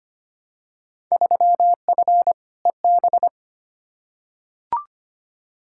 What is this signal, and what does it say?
Morse code, heard as a keyed tone, about 25 wpm, 700 Hz. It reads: 3F EB
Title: Field Day Exchanges - 25wpm